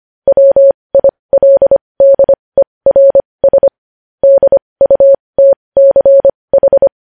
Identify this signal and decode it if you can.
Morse code, heard as a keyed tone, about 25 wpm, spaced as for 20 wpm, 550 Hz. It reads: WILDERS DUTCH
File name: podcast / News headlines-1x-25x20